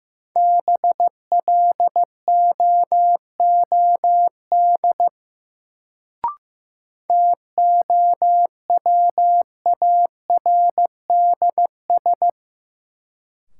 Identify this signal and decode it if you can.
Morse code, heard as a keyed tone, about 15 wpm, 700 Hz. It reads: BLOOD TOWARDS
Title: Sentences from Top 1100 Words - New Word Drill Alt - 15wpm